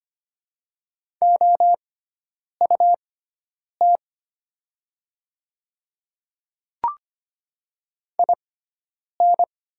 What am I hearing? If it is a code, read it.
Morse code, heard as a keyed tone, about 25 wpm, spaced as for 9 wpm, 690 Hz. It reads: OUT IN
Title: 6x Character Spacing - Top 100 Words - 25wpm